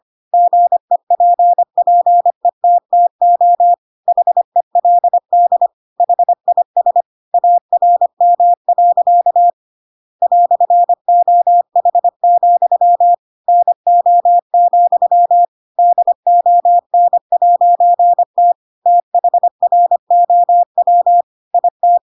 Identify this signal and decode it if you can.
Morse code, heard as a keyed tone, about 25 wpm, 700 Hz. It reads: GEPPETTO HELD HIS ARM. "OH, NO, DON'T THROW IT
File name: Pinocchio / Pinocchio ch7 25 25 ik1hge